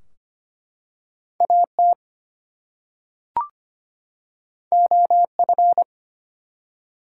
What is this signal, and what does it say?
Morse code, heard as a keyed tone, about 25 wpm, 700 Hz. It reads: AT OF
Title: Two Letter Words - 25wpm